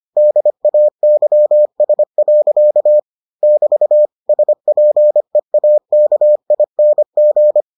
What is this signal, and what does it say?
Morse code, heard as a keyed tone, about 25 wpm, 600 Hz. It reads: DAYS. <BT> SPEAKING